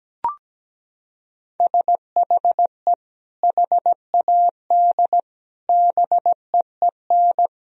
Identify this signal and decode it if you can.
Morse code, heard as a keyed tone, about 17 wpm, 700 Hz. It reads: SHE HAD BEEN
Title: Sentences from Top 1000 Words - Review Part 4 - 17wpm